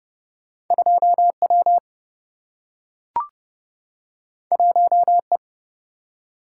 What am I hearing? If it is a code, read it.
Morse code, heard as a keyed tone, about 30 wpm, 700 Hz. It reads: 2 W 1 E
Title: Single-Letter Single-Number 30wpm